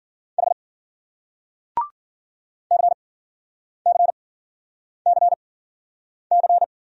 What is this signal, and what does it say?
Morse code, heard as a keyed tone, about 51 wpm, spaced as for 9 wpm, 700 Hz. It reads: H CCCC